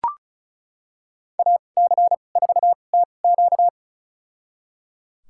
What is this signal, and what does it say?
Morse code, heard as a keyed tone, about 36 wpm, spaced as for 25 wpm, 700 Hz. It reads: AC4TQ
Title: USA Call Signs - 2x Character Spacing - 35wpm